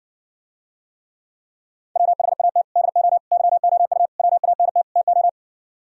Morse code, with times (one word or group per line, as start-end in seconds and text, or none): 2.0-2.6 WHAT
2.8-3.2 DO
3.3-4.1 YOU
4.2-4.8 WANT
5.0-5.3 TO